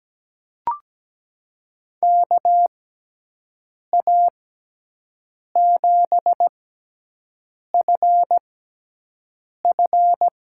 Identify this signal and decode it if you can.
Morse code, heard as a keyed tone, about 17 wpm, spaced as for 6 wpm, 700 Hz. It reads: KA7FF
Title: USA Call Signs - 6x Character Spacing - 17wpm